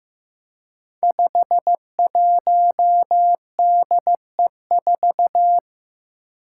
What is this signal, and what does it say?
Morse code, heard as a keyed tone, about 15 wpm, 700 Hz. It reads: 51DE4
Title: Five Challenging Letter-Number Combinations 15wpm